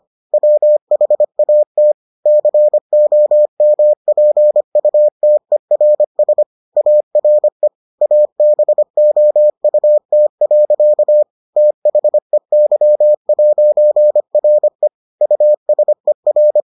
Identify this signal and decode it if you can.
Morse code, heard as a keyed tone, about 25 wpm, 600 Hz. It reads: WHAT COMPUTERS ARE ABOUT. THEY'RE USER